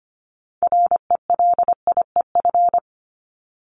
Morse code, heard as a keyed tone, about 25 wpm, 700 Hz. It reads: RELIEF
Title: Top 1501-1750 Words 25wpm